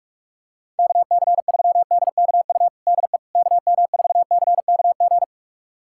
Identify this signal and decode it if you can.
Morse code, heard as a keyed tone, about 46 wpm, 700 Hz. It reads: KC3DKU DE KM4CKG
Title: USA Callsign DE Callsign 45wpm